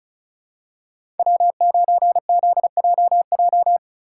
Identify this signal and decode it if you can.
Morse code, heard as a keyed tone, about 36 wpm, 700 Hz. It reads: W9ZJJ
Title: USA Call Signs - 5 Characters - 3x Character Spacing - 35wpm